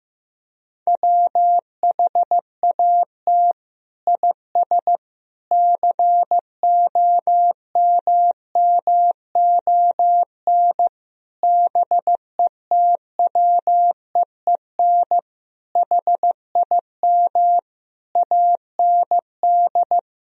Morse code, heard as a keyed tone, about 15 wpm, 700 Hz. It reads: WHAT IS COMMON BETWEEN HIM AND
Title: Sentences from Top 1000 Words - Review Part 3 - 15wpm